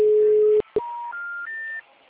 Special information tone.
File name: out- -2015